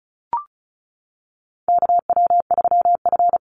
Morse code, heard as a keyed tone, about 34 wpm, 700 Hz. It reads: KW3F
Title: Call Signs - 35wpm